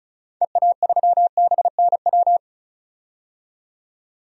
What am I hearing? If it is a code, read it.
Morse code, heard as a keyed tone, about 36 wpm, 700 Hz. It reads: EA3BNW
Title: DXCC-100 Call Signs - 6 Characters - 35wpm